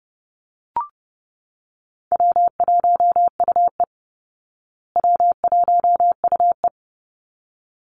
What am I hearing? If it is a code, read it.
Morse code, heard as a keyed tone, about 29 wpm, 700 Hz. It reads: W1UE W1UE
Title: Call Signs Repeated 3x - 30wpm